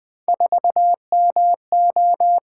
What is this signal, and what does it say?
Morse code, heard as a keyed tone, about 24 wpm, 700 Hz. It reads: 4MO
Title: Speed-Racing - Three Letter-Number Combinations - 20wpm